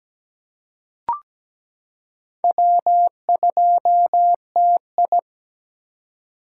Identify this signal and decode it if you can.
Morse code, heard as a keyed tone, about 17 wpm, 700 Hz. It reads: W2TI
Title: K1USN SST QSO Practice - 17wpm